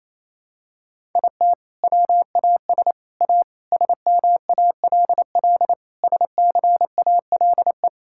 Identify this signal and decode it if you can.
Morse code, heard as a keyed tone, about 28 wpm, 700 Hz. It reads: IT WAS A SMALL SCALE
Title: Sentences from Top 1000 Words - Review Part 5 - 28wpm